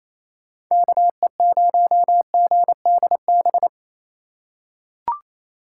Morse code, heard as a keyed tone, about 28 wpm, 700 Hz. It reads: KE0GDB
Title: USA Call Signs - 6 Characters - 3x Character Spacing - 28wpm